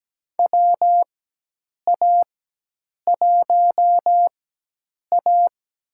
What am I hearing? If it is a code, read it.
Morse code, heard as a keyed tone, about 17 wpm, spaced as for 8 wpm, 700 Hz. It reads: WA1A
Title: USA Call Signs - 4 Characters - 4x Character Spacing - 17wpm